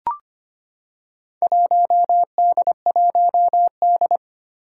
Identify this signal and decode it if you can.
Morse code, heard as a keyed tone, about 25 wpm, 700 Hz. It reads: 1D1D
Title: Four Challenging Letter-Number Combinations 25wpm